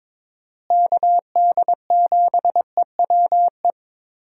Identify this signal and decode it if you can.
Morse code, heard as a keyed tone, about 22 wpm, 700 Hz. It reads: KD7EWE
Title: USA Call Signs - 6 Characters - Repeated 3x - 22wpm